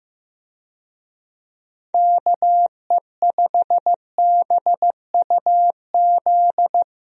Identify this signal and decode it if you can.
Morse code, heard as a keyed tone, about 15 wpm, 700 Hz. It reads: KE5BUZ
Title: USA Call Signs - 6 Characters - Repeated 2x - 15wpm